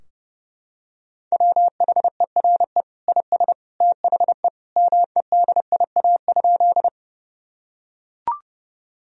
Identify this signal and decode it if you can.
Morse code, heard as a keyed tone, about 32 wpm, 700 Hz. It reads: WHERE IS THE MEDIA?
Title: Sentences from Top 1000 Words - Review Part 4 - 30wpm